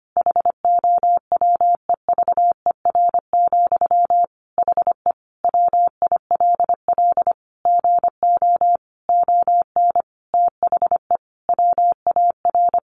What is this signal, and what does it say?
Morse code, heard as a keyed tone, about 25 wpm, 700 Hz. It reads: HOWEVER, HE WILL GO ON THE WAR
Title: Sentences from Top 1400 Words 25wpm